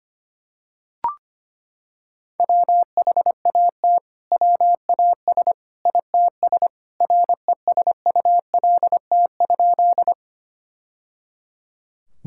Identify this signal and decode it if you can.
Morse code, heard as a keyed tone, about 25 wpm, 700 Hz. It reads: WHAT WAS ITS RESULT?
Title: Sentences from Top 1000 Words - Review Part 5 - 25wpm